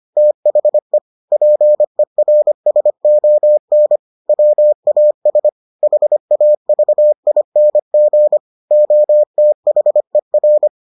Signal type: Morse code

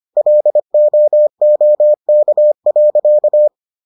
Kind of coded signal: Morse code

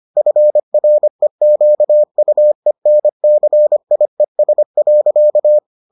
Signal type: Morse code

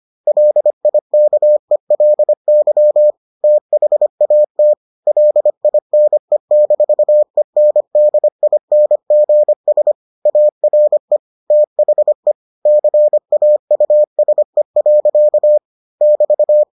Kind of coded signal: Morse code

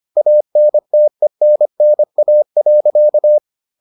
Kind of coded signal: Morse code

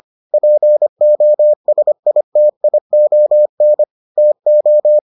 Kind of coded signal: Morse code

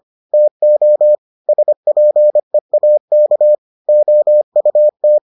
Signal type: Morse code

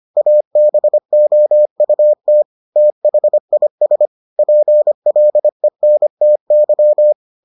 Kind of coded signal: Morse code

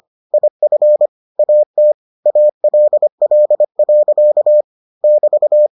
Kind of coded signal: Morse code

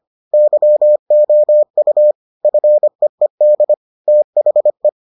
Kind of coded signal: Morse code

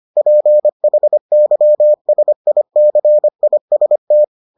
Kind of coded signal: Morse code

